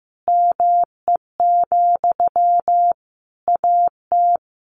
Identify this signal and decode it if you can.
Morse code, heard as a keyed tone, about 14 wpm, 700 Hz. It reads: ME, AT
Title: Sentences from Top 700 Words 15wpm